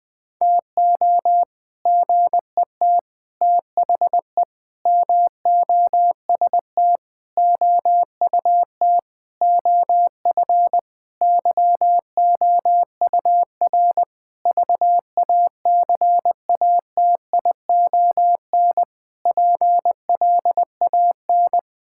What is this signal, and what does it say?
Morse code, heard as a keyed tone, about 20 wpm, 700 Hz. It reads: TO GET THE MOST OUT OF YOUR VACATION PLAN
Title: Sentences with 7-Word Phrases - Spoken Beginning - 20wpm